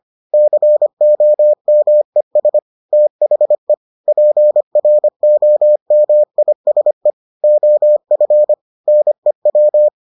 Morse code, heard as a keyed tone, about 25 wpm, 600 Hz. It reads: COMES THE PROMISE OF NEW